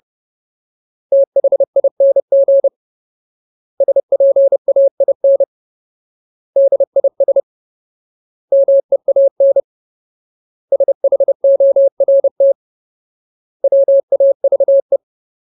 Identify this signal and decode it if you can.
Morse code, heard as a keyed tone, about 30 wpm, 550 Hz. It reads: THING SPAIN DIS MEAN SHORT WAVE
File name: Ultimate Word List / 30WPM Ultimate